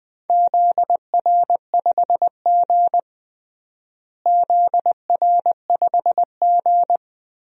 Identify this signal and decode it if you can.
Morse code, heard as a keyed tone, about 20 wpm, 700 Hz. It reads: ZR5G ZR5G
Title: DXCC-100 Call Signs - 4 Characters - Repeated 3x - 20wpm